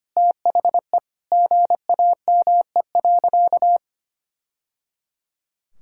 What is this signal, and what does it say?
Morse code, heard as a keyed tone, about 25 wpm, 700 Hz. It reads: THE GAME.
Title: Sentences from Top 1000 Words - Review Part 5 - 25wpm